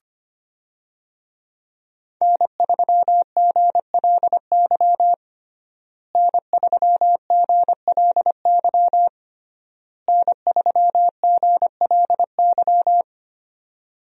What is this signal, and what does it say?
Morse code, heard as a keyed tone, about 25 wpm, 700 Hz. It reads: N3GLY N3GLY N3GLY
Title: USA Call Signs - 5 Characters - Repeated 3x - 25wpm